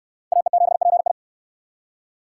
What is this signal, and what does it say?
Morse code, heard as a keyed tone, about 63 wpm, 700 Hz. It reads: AE7PI